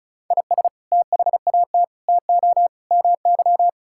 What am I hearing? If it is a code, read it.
Morse code, heard as a keyed tone, about 36 wpm, 700 Hz. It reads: IS THAT TO MY
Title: Sentences from Top 1000 Words - Review Part 6 - 35wpm